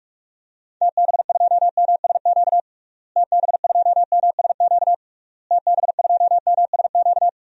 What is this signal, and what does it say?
Morse code, heard as a keyed tone, about 46 wpm, 700 Hz. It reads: TB2MSQ TB2MSQ TB2MSQ
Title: DXCC-100 Call Signs - 6 Characters - Repeated 3x - 45wpm